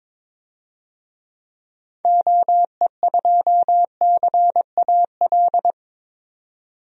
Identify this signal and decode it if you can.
Morse code, heard as a keyed tone, about 21 wpm, 700 Hz. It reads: OE2CAL
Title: DXCC-100 Call Signs - 6 Characters - 22wpm